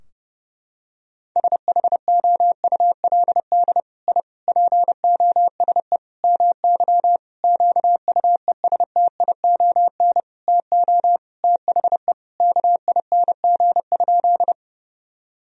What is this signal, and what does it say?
Morse code, heard as a keyed tone, about 32 wpm, 700 Hz. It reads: SHOULD I POSE MY QUESTION TO THE KING?
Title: Sentences from Top 1000 Words - Review Part 2 - 30wpm